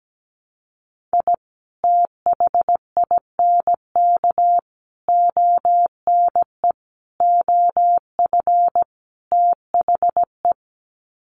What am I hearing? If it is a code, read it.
Morse code, heard as a keyed tone, about 17 wpm, 700 Hz. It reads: I THINK ONE OF THE
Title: Sets of 5 Words 17wpm